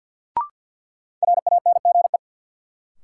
Morse code, heard as a keyed tone, about 51 wpm, 700 Hz. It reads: RANGE
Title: Sentences from Top 900 Words - New Word Drill - 50wpm